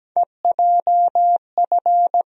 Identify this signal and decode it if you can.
Morse code, heard as a keyed tone, about 17 wpm, 700 Hz. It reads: EJF
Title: USA Call Sign Postfixes - 3 Letters 17wpm